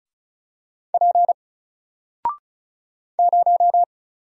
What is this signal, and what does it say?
Morse code, heard as a keyed tone, about 33 wpm, 700 Hz. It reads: P 0
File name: Single-Letter-Number-35wpm